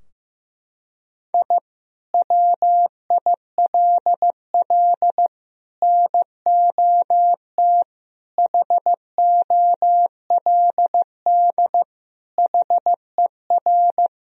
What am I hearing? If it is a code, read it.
Morse code, heard as a keyed tone, about 16 wpm, 700 Hz. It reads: I WILL NOT HOLD HER
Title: Sentences from Top 1000 Words - Review Part 2 - 15wpm